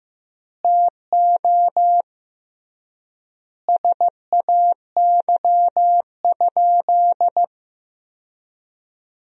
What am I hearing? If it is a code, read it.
Morse code, heard as a keyed tone, about 15 wpm, 700 Hz. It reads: TO SAY?
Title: 3x Word-Spacing - Sentences from Top 200 Words - 15wpm